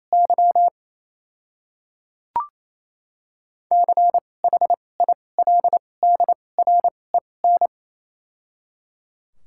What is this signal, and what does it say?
Morse code, heard as a keyed tone, about 28 wpm, spaced as for 20 wpm, 700 Hz. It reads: Y CHILDREN
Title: 2x Character Spacing - Top 300 Words - 28wpm